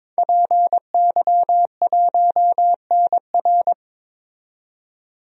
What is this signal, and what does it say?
Morse code, heard as a keyed tone, about 22 wpm, 700 Hz. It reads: PY1NR
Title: DXCC-100 Call Signs - 5 Characters - 22wpm